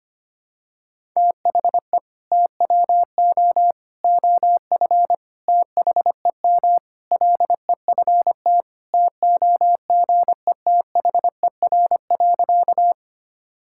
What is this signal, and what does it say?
Morse code, heard as a keyed tone, about 25 wpm, 700 Hz. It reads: THE TWO OF THEM LEFT TOGETHER.
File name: practice / 25wpm / Sentences with 4-Word Phrases - 25wpm